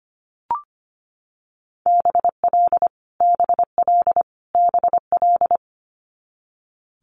Morse code, heard as a keyed tone, about 25 wpm, 700 Hz. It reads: BL BL BL
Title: Multi-letter Phonemes Repeated 3x 25wpm